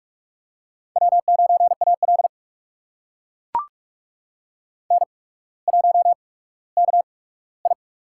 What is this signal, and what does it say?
Morse code, heard as a keyed tone, about 44 wpm, spaced as for 19 wpm, 700 Hz. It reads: W9AL N1KI
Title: USA Call Signs - 8x Character Spacing - 45wpm